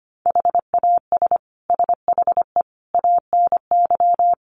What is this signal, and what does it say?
Morse code, heard as a keyed tone, about 25 wpm, 700 Hz. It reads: HAS SHE ANY
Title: Sentences from Top 400 Words 25wpm